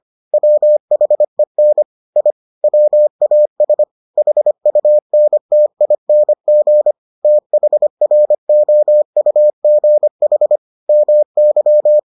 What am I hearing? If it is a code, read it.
Morse code, heard as a keyed tone, about 25 wpm, 600 Hz. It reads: WHEN I WAS HUNTING THROUGH MY